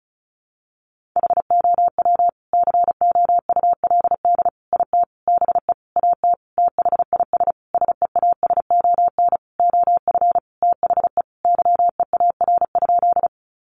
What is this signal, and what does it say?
Morse code, heard as a keyed tone, about 34 wpm, 700 Hz. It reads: HOW COULD IT BE AT THIS SEASON OF THE YEAR?
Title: Sentences from Top 1400 Words 35wpm